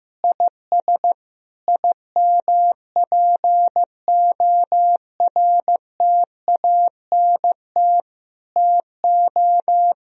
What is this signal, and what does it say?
Morse code, heard as a keyed tone, about 15 wpm, 700 Hz. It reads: IS IMPORTANT TO
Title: Sentences from Top 1000 Words - Review Part 4 - 15wpm